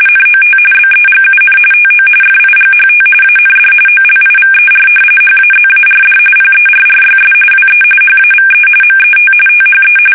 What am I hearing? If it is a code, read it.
RTTY, about 45.45 baud. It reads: NOT BEGUN TO FIGHT AND OLIVER HAZARD PERRYS WE HAVE MET THE
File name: fsk850 / battleshipnj